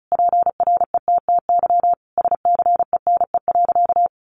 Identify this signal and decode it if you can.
Morse code, heard as a keyed tone, about 34 wpm, 700 Hz. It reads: PRETTY SCENE.
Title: Sentences from Top 1100 Words 35wpm